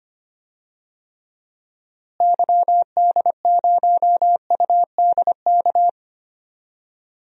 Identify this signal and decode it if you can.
Morse code, heard as a keyed tone, about 25 wpm, 700 Hz. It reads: YD0UDK
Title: DXCC-100 Call Signs - 6 Characters - Repeated 3x - 25wpm